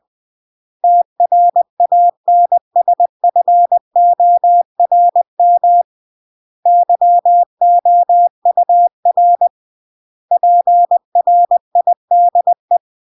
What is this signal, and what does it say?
Morse code, heard as a keyed tone, about 20 wpm, 700 Hz. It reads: TRANSFORM YOUR PRIDE